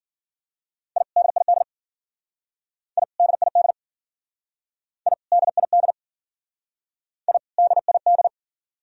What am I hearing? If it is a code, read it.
Morse code, heard as a keyed tone, about 52 wpm, 700 Hz. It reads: I DID I DID I DID I DID